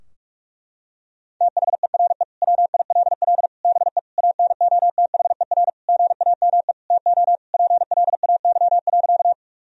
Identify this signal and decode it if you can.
Morse code, heard as a keyed tone, about 43 wpm, 700 Hz. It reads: THERE WILL BE ANOTHER GAME TO PLAY.
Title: Sentences from Top 1000 Words - Review Part 3 - 45wpm